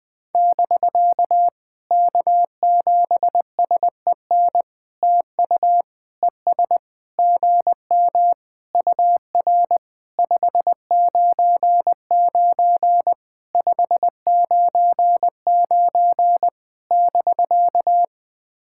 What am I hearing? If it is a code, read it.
Morse code, heard as a keyed tone, about 20 wpm, 700 Hz. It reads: <BK> K7SEN TU ES GM UR 599 599 <BK>
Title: POTA QSO Practice - 20wpm